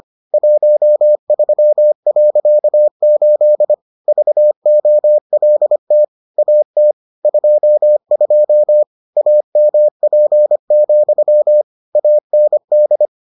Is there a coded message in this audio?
Yes: Morse code from 0.3 to 13.1 s, heard as a keyed tone, about 25 wpm, 600 Hz. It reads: 13.8 VOLT AT 22 AMP, AND